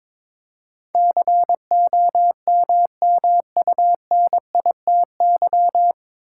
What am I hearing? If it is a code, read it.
Morse code, heard as a keyed tone, about 22 wpm, 700 Hz. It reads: COMMUNITY
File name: 9 Letter Words - First 3 Letters Given - 22wpm